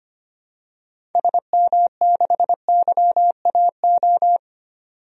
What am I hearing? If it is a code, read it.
Morse code, heard as a keyed tone, about 25 wpm, 700 Hz. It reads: SM6YAO
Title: DXCC-100 Call Signs - 6 Characters - 25wpm